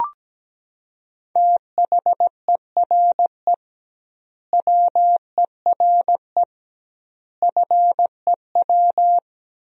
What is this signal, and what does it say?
Morse code, heard as a keyed tone, about 17 wpm, 700 Hz. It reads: THERE WERE FEW
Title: 2x Word-Spacing - Sentences from Top 300 Words - 17wpm